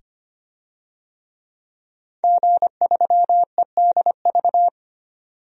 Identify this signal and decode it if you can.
Morse code, heard as a keyed tone, about 25 wpm, 700 Hz. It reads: G3EDV
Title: DXCC-100 Call Signs - 5 Characters - 25wpm